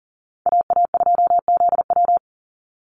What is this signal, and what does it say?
Morse code, heard as a keyed tone, about 40 wpm, 700 Hz. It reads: AA2ZW
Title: Call Signs Repeated 2x - 40wpm